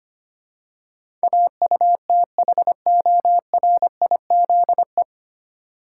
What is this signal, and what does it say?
Morse code, heard as a keyed tone, about 25 wpm, 700 Hz. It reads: AUTHORIZE
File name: Nine Letter Words - 25wpm